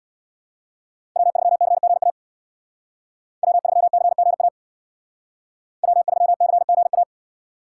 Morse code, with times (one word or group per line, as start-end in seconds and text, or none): 1.2-2.1 W3CGA
3.4-4.5 W3CGA
5.8-7.0 W3CGA